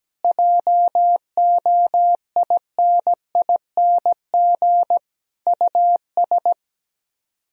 Morse code, heard as a keyed tone, about 17 wpm, 700 Hz. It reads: JOINING US
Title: Speed-Racing - Sets of 5 Words 17wpm